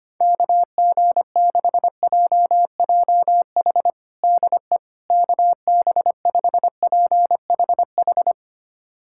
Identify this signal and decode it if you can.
Morse code, heard as a keyed tone, about 25 wpm, 700 Hz. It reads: KG6JJH DE KB5PHH
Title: USA Callsign DE Callsign 25wpm